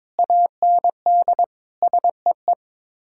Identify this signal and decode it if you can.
Morse code, heard as a keyed tone, about 22 wpm, 700 Hz. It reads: AND SEE
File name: practice / 22wpm / Sets of 7 Words - 22wpm